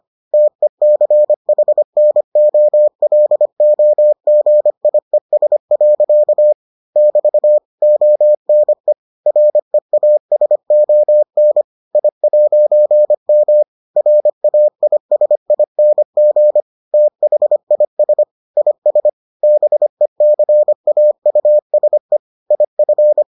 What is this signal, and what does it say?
Morse code, heard as a keyed tone, about 25 wpm, 600 Hz. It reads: TECHNOLOGIES. <BT> ONE REASON I'M RAISING THIS IS BECAUSE IF